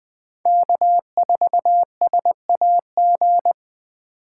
Morse code, heard as a keyed tone, about 20 wpm, 700 Hz. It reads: K4SAG
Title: USA Call Signs - 6x Character Spacing - 20wpm